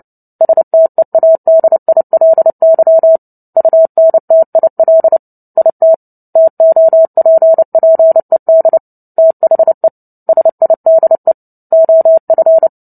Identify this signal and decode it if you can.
Morse code, heard as a keyed tone, about 30 wpm, 650 Hz. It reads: STEADILY UNTIL IT TOPPED THE SIDE OF
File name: Book Two / 03-Chapter 3 - 29 WPM